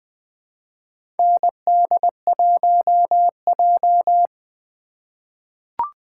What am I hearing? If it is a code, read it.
Morse code, heard as a keyed tone, about 20 wpm, 700 Hz. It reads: ND1J
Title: 2023 SOTA Chaser Call Signs - 20wpm